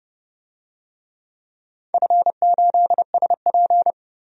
Morse code, heard as a keyed tone, about 30 wpm, 700 Hz. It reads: F8SP
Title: DXCC-100 Call Signs - 4 Characters - 30wpm